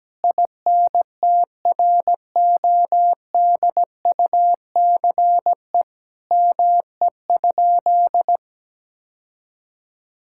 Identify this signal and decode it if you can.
Morse code, heard as a keyed tone, about 17 wpm, 700 Hz. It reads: INTRODUCE ME?
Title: Sentences from Top 1600 Words 17wpm